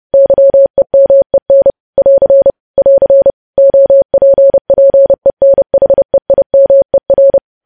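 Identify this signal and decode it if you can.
Morse code, heard as a keyed tone, about 30 wpm, 550 Hz. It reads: YEMEN <AR> <AR> OPPENHEIMER
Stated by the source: News headlines sent at 30 wpm.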